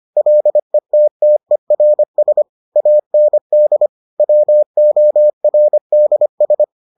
Morse code, heard as a keyed tone, about 25 wpm, 600 Hz. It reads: LETTERS AND WORDS